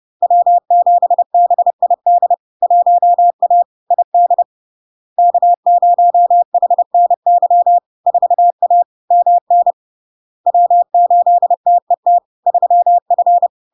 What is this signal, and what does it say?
Morse code, heard as a keyed tone, about 30 wpm, 700 Hz. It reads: W7BID 1A ID K0HNY 4A MN W8TET 3F
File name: LARK-FD-2022-Log-in-CW-30wpm-700-Hz